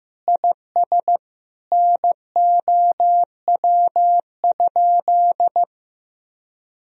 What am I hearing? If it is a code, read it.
Morse code, heard as a keyed tone, about 15 wpm, 700 Hz. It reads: IS NOW?
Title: Sentences from Top 1000 Words - Review Part 2 - 15wpm